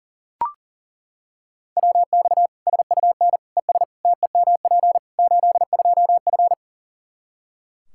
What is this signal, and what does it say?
Morse code, heard as a keyed tone, about 40 wpm, 700 Hz. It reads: WX SUN ES TEMP 82F
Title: K1USN SST QSO Practice 40wpm